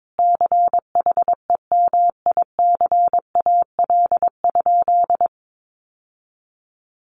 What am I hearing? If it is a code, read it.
Morse code, heard as a keyed tone, about 22 wpm, 700 Hz. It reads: CHEMICAL?
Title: Sentences from Top 1200 Words 22wpm